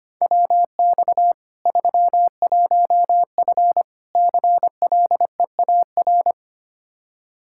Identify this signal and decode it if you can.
Morse code, heard as a keyed tone, about 25 wpm, 700 Hz. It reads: WX 31F CLEAR
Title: Practice QSO - Exchange 2 - Info - 25wpm